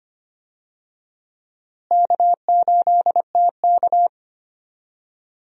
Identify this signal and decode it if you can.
Morse code, heard as a keyed tone, about 25 wpm, 700 Hz. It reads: K8TK
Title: USA Call Signs - 4 Characters - Repeated 3x - 25wpm